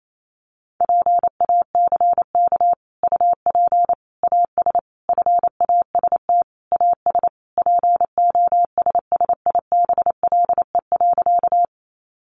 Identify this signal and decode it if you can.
Morse code, heard as a keyed tone, about 28 wpm, 700 Hz. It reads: PACK UP AS FAST AS POSSIBLE.
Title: Sentences from Top 1300 Words 28wpm